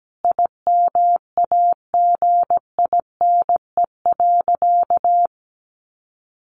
Morse code, heard as a keyed tone, about 17 wpm, 700 Hz. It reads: IMAGINE.
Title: Sentences from Top 1400 Words 17wpm